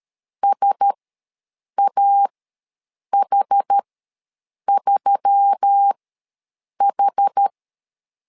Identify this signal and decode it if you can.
Morse code, heard as a keyed tone, about 13 wpm, spaced as for 7 wpm, 800 Hz. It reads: SAH3H
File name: LEZIONI / 4 Q1S3 ws 12wpm 2038